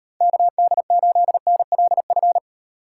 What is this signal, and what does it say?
Morse code, heard as a keyed tone, about 38 wpm, 700 Hz. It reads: KD8NLF